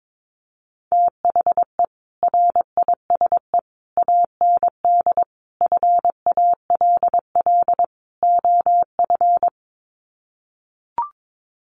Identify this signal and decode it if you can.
Morse code, heard as a keyed tone, about 22 wpm, 700 Hz. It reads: THE RISE AND FALL OF
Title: Sets of 5 Words Encore2 22wpm